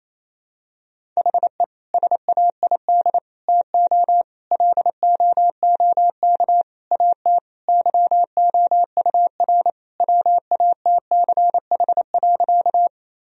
Morse code, heard as a keyed tone, about 28 wpm, 700 Hz. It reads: HE SAID TO LOOK AT YOUR WATCH.
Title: Sentences from Top 1000 Words - Review Part 1 - 28wpm